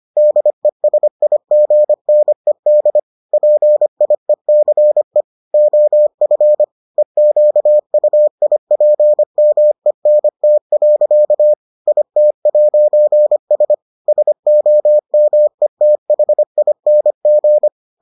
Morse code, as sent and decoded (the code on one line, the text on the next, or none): -.. ... .. --. -. -.. / .--. .. -.-. / --- ..-. / . --.- ..- .. .--. -- . -. - .-.-.- / .. - .----. ... / ... --- -- . - .... .. -. --.
DESIGNED PIECE OF EQUIPMENT. IT'S SOMETHING